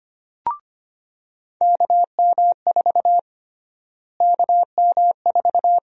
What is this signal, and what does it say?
Morse code, heard as a keyed tone, about 25 wpm, 700 Hz. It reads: KM4 KM4
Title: USA Call Sign Prefixes - Repeated 2x - 25wpm